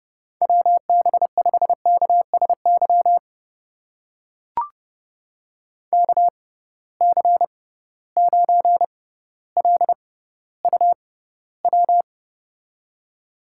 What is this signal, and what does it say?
Morse code, heard as a keyed tone, about 30 wpm, spaced as for 15 wpm, 700 Hz. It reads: WB5KSY KC9LUW
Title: USA Call Signs - 6 Characters - 6x Character Spacing - 30wpm